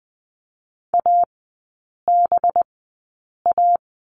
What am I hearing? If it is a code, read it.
Morse code, heard as a keyed tone, about 20 wpm, spaced as for 8 wpm, 700 Hz. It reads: ABA
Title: 184 Lesson 30e - Letter b - Three Characters 20wpm